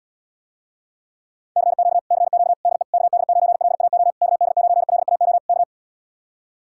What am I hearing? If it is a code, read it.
Morse code, heard as a keyed tone, about 77 wpm, 700 Hz. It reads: CQ CQ DE KD9LAO KD9LAO K